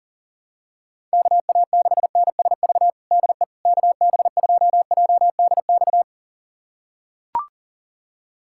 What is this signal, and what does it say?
Morse code, heard as a keyed tone, about 40 wpm, 700 Hz. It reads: KA6NSV DE KB2JDX
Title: USA Callsign DE Callsign 40wpm